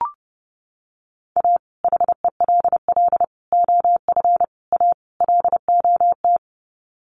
Morse code, heard as a keyed tone, about 30 wpm, 700 Hz. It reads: A HELL OF A LOT
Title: Sets of 5 Words Encore 30wpm